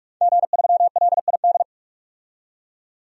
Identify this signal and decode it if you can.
Morse code, heard as a keyed tone, about 47 wpm, 700 Hz. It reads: G3LID
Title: DXCC-100 Call Signs - 5 Characters - 45wpm